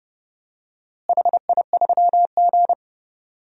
Morse code, heard as a keyed tone, about 29 wpm, 700 Hz. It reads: HI3G
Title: DXCC-100 Call Signs - 4 Characters - Repeated 2x - 30wpm